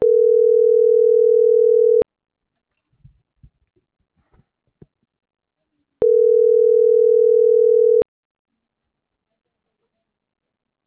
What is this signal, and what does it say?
Ringback tone (North America), 440 + 480 Hz.